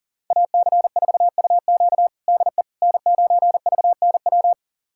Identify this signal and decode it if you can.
Morse code, heard as a keyed tone, about 40 wpm, 690 Hz. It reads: AC4UQ DE N9VNW
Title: USA Callsign DE Callsign - 5 Characters - 40wpm